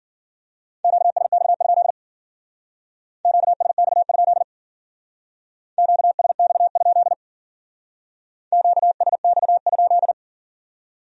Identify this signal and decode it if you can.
Morse code, heard as a keyed tone, about 50 wpm, 700 Hz. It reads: QSX? QSX? QSX? QSX?